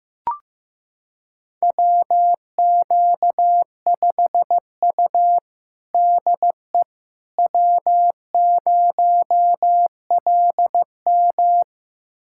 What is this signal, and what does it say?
Morse code, heard as a keyed tone, about 15 wpm, 700 Hz. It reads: WQ5U DE W0LM
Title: USA Callsign DE Callsign - 4 Characters - 15wpm